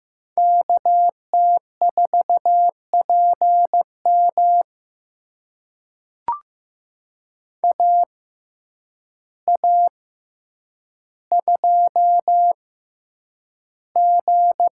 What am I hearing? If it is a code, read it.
Morse code, heard as a keyed tone, about 15 wpm, spaced as for 8 wpm, 700 Hz. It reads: KT4PM AA2G
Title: USA Call Signs - 5 Characters - 6x Character Spacing - 15wpm